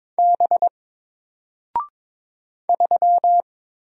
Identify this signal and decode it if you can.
Morse code, heard as a keyed tone, about 22 wpm, 700 Hz. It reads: B 3
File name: Single-Letter-Number-22wpm